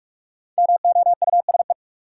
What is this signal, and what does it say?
Morse code, heard as a keyed tone, about 44 wpm, 700 Hz. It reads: MOUSE
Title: Sentences from Top 700 Words - New Word Drill - 45wpm